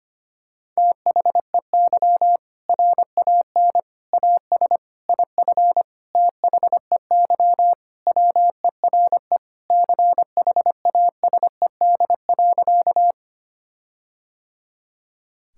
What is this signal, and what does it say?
Morse code, heard as a keyed tone, about 25 wpm, 700 Hz. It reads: THEY RAN AS IF THEY WERE CHASED.
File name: practice / 25wpm / Sentences with 4-Word Phrases - 25wpm